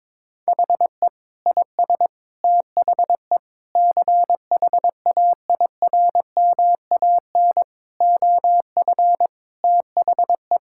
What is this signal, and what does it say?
Morse code, heard as a keyed tone, about 22 wpm, 700 Hz. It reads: HE IS THE CHAIRMAN OF THE
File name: practice / 22wpm / Sentences with 3-Word Phrases - 22wpm